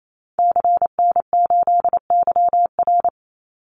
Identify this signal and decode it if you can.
Morse code, heard as a keyed tone, about 28 wpm, 700 Hz. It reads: CN8YR
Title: Call Signs Repeated 3x - 28wpm